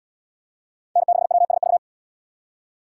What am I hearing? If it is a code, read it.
Morse code, heard as a keyed tone, about 72 wpm, 700 Hz. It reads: N6WSV